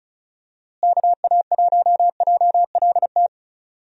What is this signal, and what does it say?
Morse code, heard as a keyed tone, about 34 wpm, 700 Hz. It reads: KA1JLT
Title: USA Call Signs - 6 Characters - Repeated 2x - 35wpm